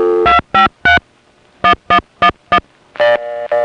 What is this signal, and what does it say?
DTMF, heard as telephone touch tones, tones of about 100 ms. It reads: 253 1111
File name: phonetrips / SoCal1HQ